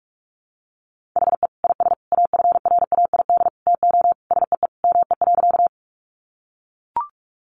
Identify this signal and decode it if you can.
Morse code, heard as a keyed tone, about 45 wpm, 700 Hz. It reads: HE IS AFRAID TO SEE ME.
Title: Sentences from Top 1500 Words 45wpm